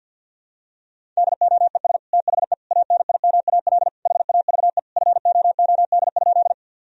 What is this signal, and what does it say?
Morse code, heard as a keyed tone, about 52 wpm, 700 Hz. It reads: DOES THE ANIMAL HAVE FOOD?
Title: Sentences from Top 1000 Words - Review Part 1 - 50wpm